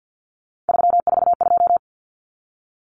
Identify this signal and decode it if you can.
Morse code, heard as a keyed tone, about 50 wpm, 700 Hz. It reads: 342
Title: Numbers - 3-Digit 50wpm